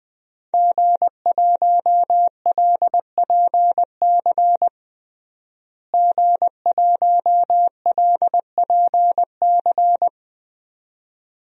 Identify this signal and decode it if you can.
Morse code, heard as a keyed tone, about 20 wpm, 700 Hz. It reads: G1LPC G1LPC
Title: DXCC-100 Call Signs - 5 Characters - Repeated 2x - 20wpm